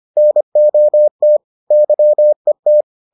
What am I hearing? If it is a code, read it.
Morse code, heard as a keyed tone, about 25 wpm, 600 Hz. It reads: NOT YET